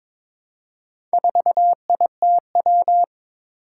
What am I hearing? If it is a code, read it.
Morse code, heard as a keyed tone, about 22 wpm, 700 Hz. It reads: 4ITW
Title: Four Challenging Letter-Number Combinations 22wpm